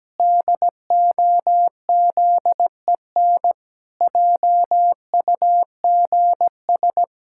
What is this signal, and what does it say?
Morse code, heard as a keyed tone, about 17 wpm, 700 Hz. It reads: DOZEN JUGS
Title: Pangram Challenge v2 - 17wpm